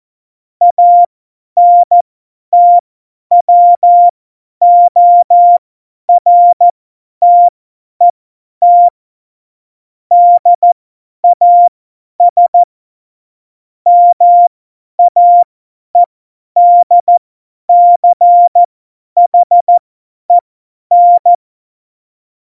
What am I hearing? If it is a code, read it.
Morse code, heard as a keyed tone, about 14 wpm, spaced as for 10 wpm, 700 Hz. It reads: ANTWORTET DAS MAEDCHEN
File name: Morsekurs / Witze / Witz 22. Januar 2019, 50 F70